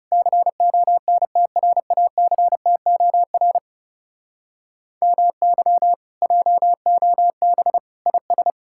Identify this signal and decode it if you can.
Morse code, heard as a keyed tone, about 37 wpm, 700 Hz. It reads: CONTRACTOR MY JOB IS
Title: Speed-Racing Common QSO Phrases - 30wpm